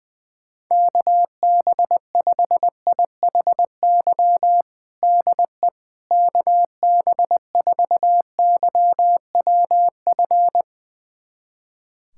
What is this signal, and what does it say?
Morse code, heard as a keyed tone, about 20 wpm, 700 Hz. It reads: KB5IHY DE KB4YWF
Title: USA Callsign DE Callsign - 6 Characters - 20wpm